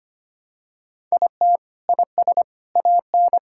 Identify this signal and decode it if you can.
Morse code, heard as a keyed tone, about 25 wpm, 700 Hz. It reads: IT IS AN
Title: Sentences from Top 1000 Words - Review Part 5 - 25wpm